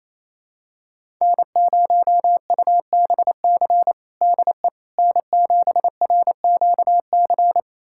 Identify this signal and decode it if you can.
Morse code, heard as a keyed tone, about 27 wpm, 700 Hz. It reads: N0UBC DE N7RQC
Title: USA Callsign DE Callsign - 5 Characters - 28wpm